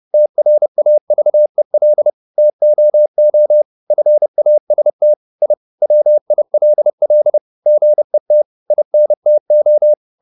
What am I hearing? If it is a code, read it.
Morse code, heard as a keyed tone, about 30 wpm, 600 Hz. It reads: TRAVEL TOO FAST I WILL GET INTO
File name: the old man and the sea / the old man and the sea 1 30wpm